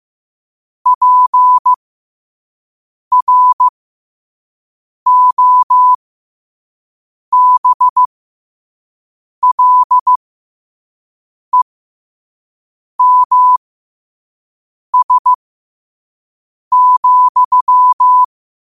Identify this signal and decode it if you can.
Morse code, heard as a keyed tone, about 15 wpm, spaced as for 5 wpm, 1000 Hz. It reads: PROBLEMS,
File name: qotdfiles / 5wpm